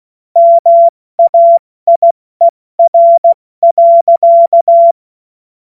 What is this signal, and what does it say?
Morse code, heard as a keyed tone, about 16 wpm, 680 Hz. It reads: MAIER.
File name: Morsekurs / Witze / Witz 13. Juni 2023 70 F80